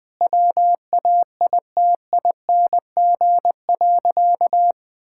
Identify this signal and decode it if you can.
Morse code, heard as a keyed tone, about 20 wpm, 700 Hz. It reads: WAITING.
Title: Sentences with 5-Word Phrases - 20wpm